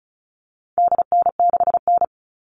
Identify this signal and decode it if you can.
Morse code, heard as a keyed tone, about 34 wpm, 700 Hz. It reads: DN6N
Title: All Cut Numbers - 4-Digit 35wpm